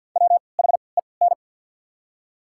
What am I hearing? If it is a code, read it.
Morse code, heard as a keyed tone, about 54 wpm, spaced as for 29 wpm, 690 Hz. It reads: WHEN
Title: 3x Character Spacing - Top 200 Words - 50wpm